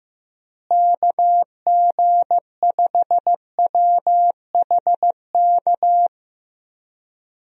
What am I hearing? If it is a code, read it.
Morse code, heard as a keyed tone, about 15 wpm, 700 Hz. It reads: KG5WHK
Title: USA Call Signs - 6 Characters - Repeated 2x - 15wpm